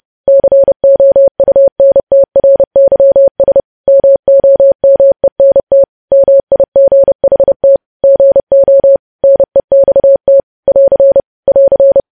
Morse code, heard as a keyed tone, about 30 wpm, 550 Hz. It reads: COUNTRYS MOMENT MIGHT GO NEXT <AR> <AR>
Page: news headlines in morse code at 30 wpm